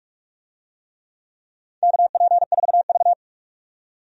Morse code, heard as a keyed tone, about 45 wpm, 700 Hz. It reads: KP4V
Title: DXCC-100 Call Signs - 4 Characters - 45wpm